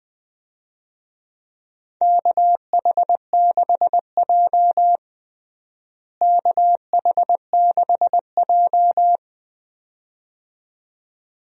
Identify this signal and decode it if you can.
Morse code, heard as a keyed tone, about 21 wpm, 700 Hz. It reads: KH6J KH6J
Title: DXCC-100 Call Signs - 4 Characters - Repeated 2x - 20wpm